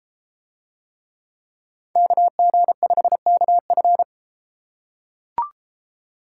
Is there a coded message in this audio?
Yes: Morse code from 2.0 to 4.0 s, heard as a keyed tone, about 33 wpm, 700 Hz. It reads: KG5KF